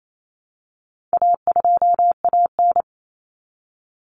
Morse code, heard as a keyed tone, about 28 wpm, 700 Hz. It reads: A2AN
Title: Common Cut Numbers - 4-Digit 28wpm